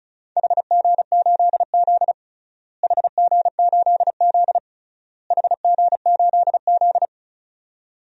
Morse code, heard as a keyed tone, about 34 wpm, 700 Hz. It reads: HG8Z HG8Z HG8Z
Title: DXCC-100 Call Signs - 4 Characters - Repeated 3x - 35wpm